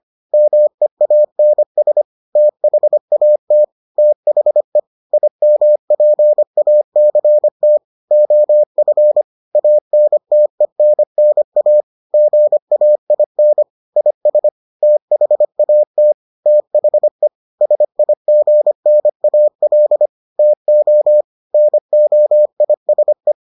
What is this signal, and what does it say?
Morse code, heard as a keyed tone, about 25 wpm, 600 Hz. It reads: MEANS THAT THE IMPACT OF ANTENNA GAIN IS THAT THE SIGNAL TO NOISE